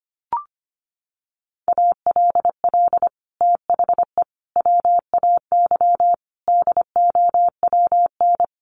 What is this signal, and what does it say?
Morse code, heard as a keyed tone, about 25 wpm, 700 Hz. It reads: ALL THE WAY DOWN
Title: Sets of 5 Words Encore2 25wpm